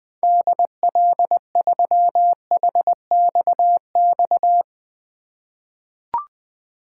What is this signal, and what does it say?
Morse code, heard as a keyed tone, about 20 wpm, 700 Hz. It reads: DL3HXX
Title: 2023 SOTA Chaser Call Signs - 20wpm